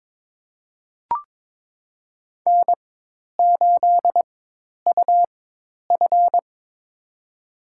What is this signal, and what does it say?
Morse code, heard as a keyed tone, about 22 wpm, spaced as for 10 wpm, 700 Hz. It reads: N8UF
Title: USA Call Signs - 4 Characters - 4x Character Spacing - 22wpm